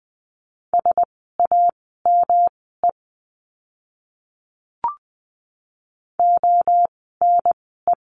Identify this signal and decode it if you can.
Morse code, heard as a keyed tone, about 20 wpm, spaced as for 15 wpm, 700 Hz. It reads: SAME ONE
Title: 146 Lesson 24f - Letter y - Words 20wpm